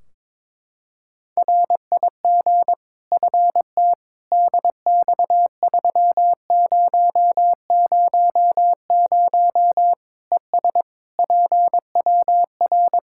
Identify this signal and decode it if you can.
Morse code, heard as a keyed tone, about 22 wpm, 700 Hz. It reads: RIG FT DX3000 ES PWR
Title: K1USN SST QSO Practice 22wpm